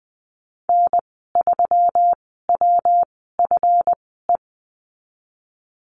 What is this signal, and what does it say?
Morse code, heard as a keyed tone, about 20 wpm, spaced as for 15 wpm, 700 Hz. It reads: N3WFE
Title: 140 Lesson 23g - Letter f - Call Signs 20wpm